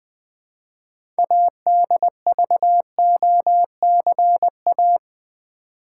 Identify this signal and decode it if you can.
Morse code, heard as a keyed tone, about 21 wpm, 700 Hz. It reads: ADVOCA
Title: Build A Word - 3 Letters At A Time - 8-Letter Words - 20wpm